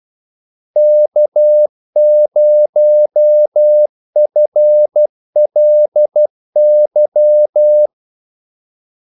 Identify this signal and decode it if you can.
Morse code, heard as a keyed tone, about 12 wpm, 600 Hz. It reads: K0FLY
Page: 2 ekstra ordmellemrum mellem kaldesignalerne